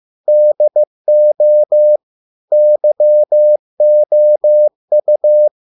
Morse code, heard as a keyed tone, about 15 wpm, 600 Hz. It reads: DO YOU